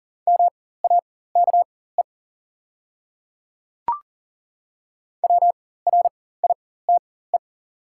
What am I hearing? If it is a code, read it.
Morse code, heard as a keyed tone, about 40 wpm, spaced as for 19 wpm, 700 Hz. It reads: MAKE WRITE
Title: 4x Character Spacing - Top 100 Words - 40wpm